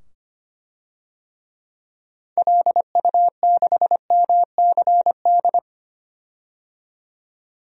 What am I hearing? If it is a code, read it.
Morse code, heard as a keyed tone, about 25 wpm, 700 Hz. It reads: LU6MCD
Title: DXCC-100 Call Signs - 6 Characters - 25wpm